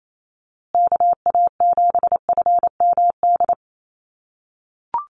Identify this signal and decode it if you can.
Morse code, heard as a keyed tone, about 28 wpm, 700 Hz. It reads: KA7FMD
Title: USA Call Signs Repeated 2x 28wpm